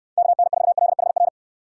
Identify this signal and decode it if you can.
Morse code, heard as a keyed tone, about 68 wpm, 700 Hz. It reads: KN4LHR